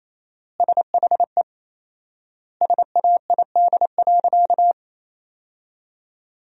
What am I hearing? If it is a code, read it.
Morse code, heard as a keyed tone, about 28 wpm, 700 Hz. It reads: SHE SAID.
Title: 4x Word-Spacing - Sentences from Top 200 Words - 28wpm